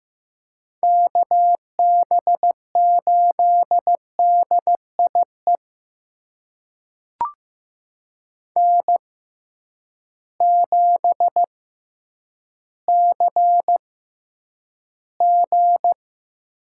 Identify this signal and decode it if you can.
Morse code, heard as a keyed tone, about 15 wpm, spaced as for 9 wpm, 700 Hz. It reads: KB8DIE N7CG
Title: USA Call Signs - 6x Character Spacing - 15wpm